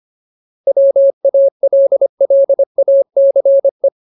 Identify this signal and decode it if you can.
Morse code, heard as a keyed tone, about 25 wpm, 550 Hz. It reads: WALLACE
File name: Ultimate Word List / 25WPM Ultimate